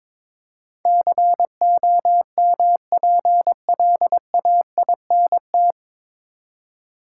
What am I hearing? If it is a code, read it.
Morse code, heard as a keyed tone, about 22 wpm, 700 Hz. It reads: COMPLAINT
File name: 9 Letter Words - First 3 Letters Given - 22wpm